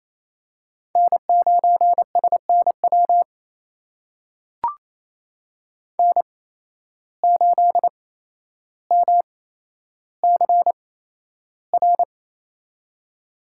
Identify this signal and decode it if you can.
Morse code, heard as a keyed tone, about 28 wpm, spaced as for 12 wpm, 700 Hz. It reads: N9SNW N8MCR
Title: USA Call Signs - 5 Characters - 8x Character Spacing - 28wpm